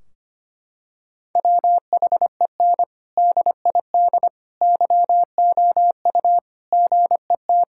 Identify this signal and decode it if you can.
Morse code, heard as a keyed tone, about 25 wpm, 700 Hz. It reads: WHEN DID YOU GET
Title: Sentences from Top 1600 Words 25wpm